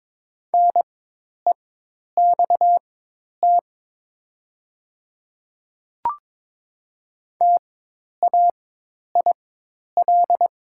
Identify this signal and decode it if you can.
Morse code, heard as a keyed tone, about 22 wpm, spaced as for 10 wpm, 700 Hz. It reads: NEXT TAIL
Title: 4x Character Spacing - Top 500 Words - 22wpm